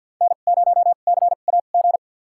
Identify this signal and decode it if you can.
Morse code, heard as a keyed tone, about 47 wpm, spaced as for 35 wpm, 700 Hz. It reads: N0CAG
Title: USA Call Signs - 5 Characters - 2x Character Spacing - 50wpm